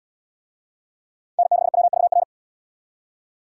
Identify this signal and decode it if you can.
Morse code, heard as a keyed tone, about 66 wpm, 700 Hz. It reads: N6WVR